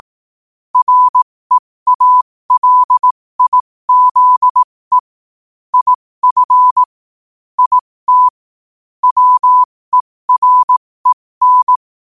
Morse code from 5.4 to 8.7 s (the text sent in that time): IF IT